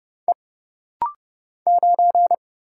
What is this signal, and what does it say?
Morse code, heard as a keyed tone, about 30 wpm, 700 Hz. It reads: E 9
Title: Single Letter-Number Alt - Rapid-Fire 30wpm